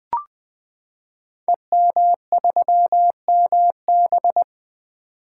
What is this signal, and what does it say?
Morse code, heard as a keyed tone, about 20 wpm, 700 Hz. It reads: EM3MB
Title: Code Groups - 5 Letter:Numbers - 20wpm